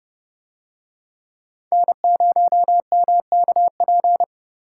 Morse code, heard as a keyed tone, about 28 wpm, 700 Hz. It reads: N0MKP